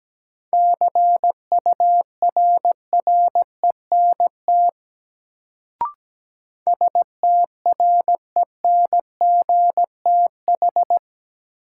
Morse code, heard as a keyed tone, about 17 wpm, 700 Hz. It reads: CURRENT STRENGTH
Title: Sentences from Top 1000 Words - Review Part 4 - New Word Drill - 17wpm